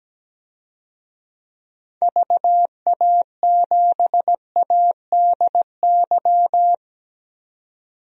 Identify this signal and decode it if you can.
Morse code, heard as a keyed tone, about 17 wpm, 700 Hz. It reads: VA7ADY
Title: DXCC-100 Call Signs - 6 Characters - 17wpm